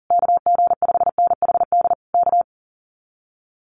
Morse code, heard as a keyed tone, about 38 wpm, 700 Hz. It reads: KG5NHD K
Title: Common QSO Phrases Encore 40wpm